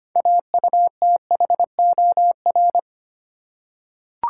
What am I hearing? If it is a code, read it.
Morse code, heard as a keyed tone, about 25 wpm, 690 Hz. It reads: AUTHOR
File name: practice / 25wpm / Easy Words with Hints - 25wpm